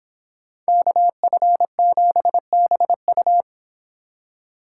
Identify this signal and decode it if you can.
Morse code, heard as a keyed tone, about 26 wpm, 700 Hz. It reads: KF7BU